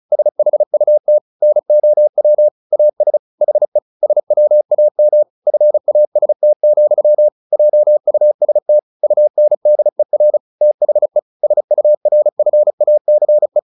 Morse code, heard as a keyed tone, about 34 wpm, 600 Hz. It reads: SHUT NOW AS HE SWAM FAST, JUST UNDER THE SURFACE
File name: the old man and the sea 2 35wpm